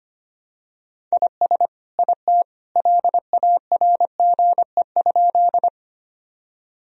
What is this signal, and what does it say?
Morse code, heard as a keyed tone, about 25 wpm, 700 Hz. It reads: IS IT LARGE?
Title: Sentences from Top 1000 Words - Review Part 1 - 25wpm